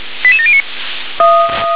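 Five-tone selective call (ZVEI), heard as quick runs of five tones, tones of about 70 ms. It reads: 88709
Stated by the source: ZVEI five-tone call for address 88084, received by SDR.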